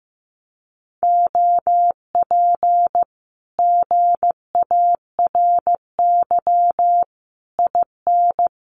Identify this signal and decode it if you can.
Morse code, heard as a keyed tone, about 14 wpm, 700 Hz. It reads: OP GARY IN
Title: Common QSO Phrases Encore 15wpm